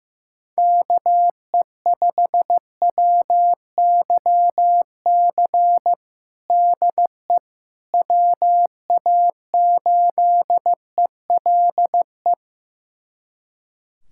Morse code, heard as a keyed tone, about 15 wpm, 700 Hz. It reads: KE5WYC DE WA8ELE
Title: USA Callsign DE Callsign - 6 Characters - 15wpm